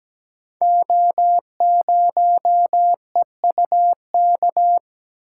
Morse code, heard as a keyed tone, about 17 wpm, 700 Hz. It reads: O0EUK
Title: Code Groups - 5 Letter:Numbers - 17wpm